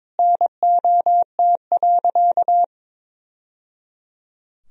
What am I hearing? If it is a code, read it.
Morse code, heard as a keyed tone, about 22 wpm, 700 Hz. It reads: NOT.
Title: Sentences from Top 1000 Words - Review Part 3 - 22wpm